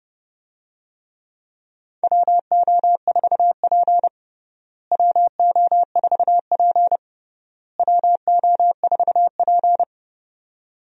Morse code, heard as a keyed tone, about 32 wpm, 700 Hz. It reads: WO4P WO4P WO4P
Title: USA Call Signs - 4 Characters - Repeated 3x - 30wpm